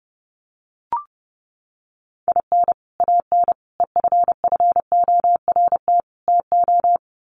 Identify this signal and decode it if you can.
Morse code, heard as a keyed tone, about 30 wpm, 700 Hz. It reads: IN AN EFFORT TO
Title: Sets of 4 Words 30wpm